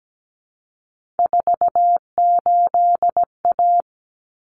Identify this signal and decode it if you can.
Morse code, heard as a keyed tone, about 17 wpm, 700 Hz. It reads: 48A
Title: Common Cut Numbers - 3-Digit 17wpm